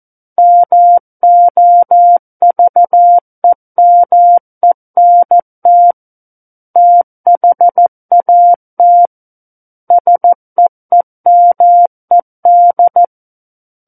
Morse code, heard as a keyed tone, about 14 wpm, 700 Hz. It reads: MOVEMENT THAT SEEMED
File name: Book One / 05-Chapter 5 - 14 WPM